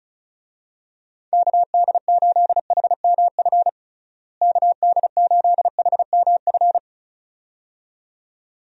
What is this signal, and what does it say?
Morse code, heard as a keyed tone, about 36 wpm, 700 Hz. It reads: KD8HMF KD8HMF
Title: USA Call Signs - 6 Characters - Repeated 2x - 35wpm